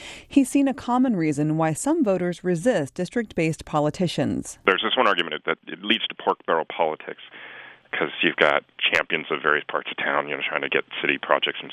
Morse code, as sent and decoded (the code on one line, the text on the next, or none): none